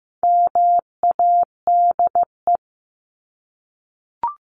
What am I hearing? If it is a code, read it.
Morse code, heard as a keyed tone, about 15 wpm, 700 Hz. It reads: MADE
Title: Top 300 Words Repeated 15wpm